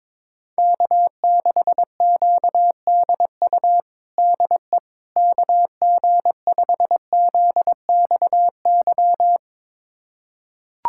Morse code, heard as a keyed tone, about 22 wpm, 700 Hz. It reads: K6QDU DE KG5ZXY
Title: USA Callsign DE Callsign 22wpm